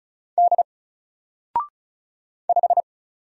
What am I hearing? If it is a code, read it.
Morse code, heard as a keyed tone, about 36 wpm, 700 Hz. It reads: D 5
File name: Single-Letter-Number-35wpm